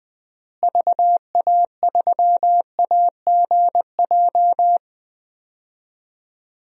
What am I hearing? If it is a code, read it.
Morse code, heard as a keyed tone, about 19 wpm, 700 Hz. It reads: VA3AGJ
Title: DXCC-100 Call Signs - 6 Characters - Repeated 3x - 20wpm